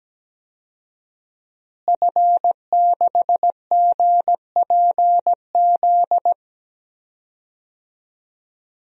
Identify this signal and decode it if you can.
Morse code, heard as a keyed tone, about 17 wpm, 700 Hz. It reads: F6GPZ
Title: DXCC-100 Call Signs - 5 Characters - Repeated 3x - 17wpm